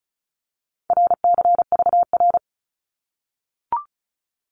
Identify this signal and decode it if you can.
Morse code, heard as a keyed tone, about 34 wpm, 700 Hz. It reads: RCVR
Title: Abbreviations Repeated 3x - 35wpm